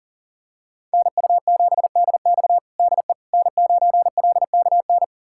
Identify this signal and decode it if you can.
Morse code, heard as a keyed tone, about 40 wpm, 700 Hz. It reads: NU7DX DE N9LKN
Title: USA Callsign DE Callsign - 5 Characters - 40wpm